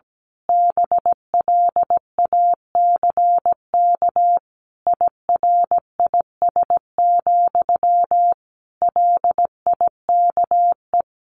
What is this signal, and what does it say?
Morse code, heard as a keyed tone, about 17 wpm, 700 Hz. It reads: BLACK IRIS, LIKE
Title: A Sound Of Thunder 17wpm